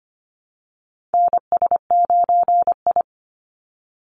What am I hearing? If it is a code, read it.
Morse code, heard as a keyed tone, about 25 wpm, 700 Hz. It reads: NS9I
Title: Call Signs - 25wpm